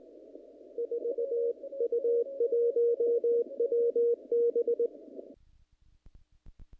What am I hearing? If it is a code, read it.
Morse code, heard as a keyed tone, about 19 wpm, 470 Hz. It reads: 4U1WB
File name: contests / naqp-cw-k5tr-op-k5pi- -1900